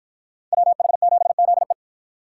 Morse code, heard as a keyed tone, about 53 wpm, 700 Hz. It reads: WH7ZE